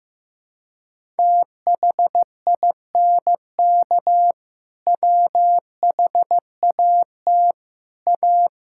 Morse code, heard as a keyed tone, about 15 wpm, 700 Hz. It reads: THINK WHAT A
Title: Sentences from Top 1000 Words - Review Part 5 - 15wpm